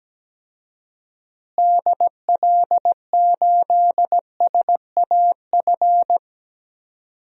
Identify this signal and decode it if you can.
Morse code, heard as a keyed tone, about 17 wpm, 700 Hz. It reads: DL8SAF
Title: DXCC-100 Call Signs - 6 Characters - Repeated 3x - 17wpm